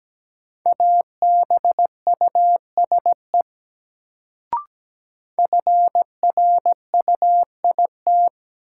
Morse code, heard as a keyed tone, about 17 wpm, 700 Hz. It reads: ABUSE FRUIT
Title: Sentences from Top 1300 Words - New Word Drill - 17wpm